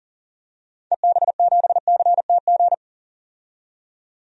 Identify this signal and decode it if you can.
Morse code, heard as a keyed tone, about 39 wpm, 700 Hz. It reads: EB7CTG
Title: DXCC-100 Call Signs - 6 Characters - 40wpm